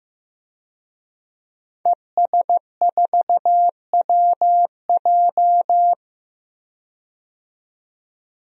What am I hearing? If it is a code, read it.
Morse code, heard as a keyed tone, about 16 wpm, 700 Hz. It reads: ES4WJ
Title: DXCC-100 Call Signs - 5 Characters - 15wpm